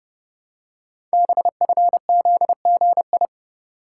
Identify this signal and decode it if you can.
Morse code, heard as a keyed tone, about 30 wpm, 700 Hz. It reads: BFZGI
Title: Code Groups - 5 Letters - 30wpm